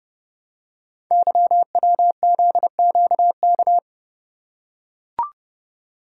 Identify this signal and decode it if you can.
Morse code, heard as a keyed tone, about 31 wpm, 700 Hz. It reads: YWZQK
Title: Code Groups - 5 Letters - 30wpm